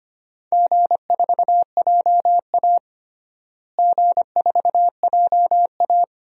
Morse code, heard as a keyed tone, about 25 wpm, 700 Hz. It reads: G4JA G4JA
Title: DXCC-100 Call Signs - 4 Characters - Repeated 2x - 25wpm